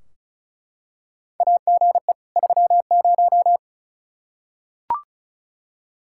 Morse code, heard as a keyed tone, about 34 wpm, 700 Hz. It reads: AGE 30
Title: Practice QSO - Exchange 3 - More Info - 35wpm